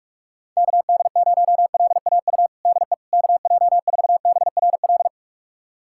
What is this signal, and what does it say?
Morse code, heard as a keyed tone, about 42 wpm, 690 Hz. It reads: KD0LAU DE KJ4BRL
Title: USA Callsign DE Callsign 45wpm